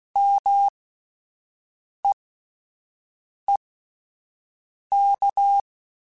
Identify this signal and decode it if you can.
Morse code, heard as a keyed tone, about 16 wpm, spaced as for 5 wpm, 800 Hz. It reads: MEEK
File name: Alphabetical / Alphabetical mp3s / Ms @5